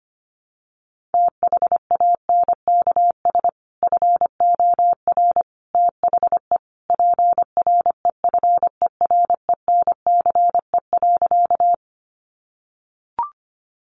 Morse code, heard as a keyed tone, about 25 wpm, 700 Hz. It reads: THANKS FOR THE PREFERENCE.
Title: Sentences from Top 1500 Words 25wpm